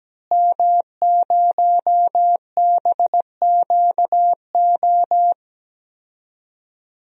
Morse code, heard as a keyed tone, about 17 wpm, 700 Hz. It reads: M0BQO
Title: DXCC-100 Call Signs - 5 Characters - Repeated 2x - 17wpm